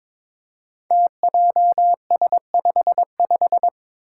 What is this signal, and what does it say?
Morse code, heard as a keyed tone, about 22 wpm, 700 Hz. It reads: TJS55
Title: Five Challenging Letter-Number Combinations 22wpm